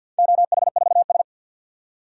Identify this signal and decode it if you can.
Morse code, heard as a keyed tone, about 50 wpm, 700 Hz. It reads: OH4S
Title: DXCC-100 Call Signs - 4 Characters - 50wpm